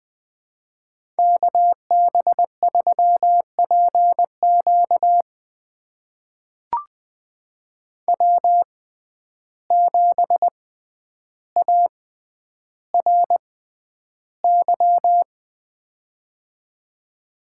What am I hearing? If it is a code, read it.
Morse code, heard as a keyed tone, about 20 wpm, spaced as for 10 wpm, 700 Hz. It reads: KB3PQ W7ARY
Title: USA Call Signs - 5 Characters - 6x Character Spacing - 20wpm